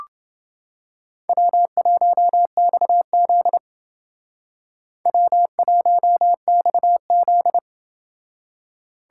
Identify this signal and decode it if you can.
Morse code, heard as a keyed tone, about 28 wpm, 700 Hz. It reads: W1XZ W1XZ